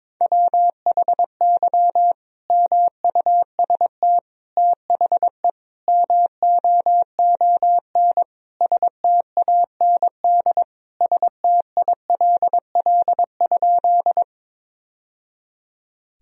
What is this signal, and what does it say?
Morse code, heard as a keyed tone, about 22 wpm, 700 Hz. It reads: WHY MUST THE MOON STAND STILL?
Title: Sentences from Top 1000 Words - Review Part 3 - 22wpm